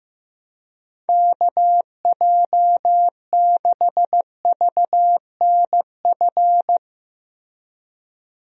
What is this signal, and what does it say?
Morse code, heard as a keyed tone, about 15 wpm, 700 Hz. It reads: KJ6VNF
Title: USA Call Signs - 6 Characters - Repeated 3x - 15wpm